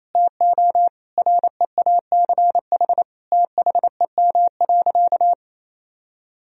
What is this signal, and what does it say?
Morse code, heard as a keyed tone, about 27 wpm, 700 Hz. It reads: TO REACH THEM.
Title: Sentences from Top 1000 Words - Review Part 2 - 28wpm